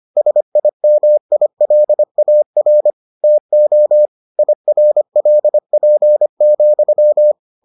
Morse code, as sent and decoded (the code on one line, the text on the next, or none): ... .. -- .. .-.. .- .-. / - --- / .. .-. .-.. .--. --..--
SIMILAR TO IRLP,